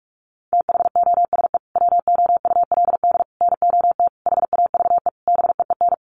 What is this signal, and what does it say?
Morse code, heard as a keyed tone, about 44 wpm, 700 Hz. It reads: THOSE WOULD NOT HAVE BEEN
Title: Sentences from Top 1300 Words 45wpm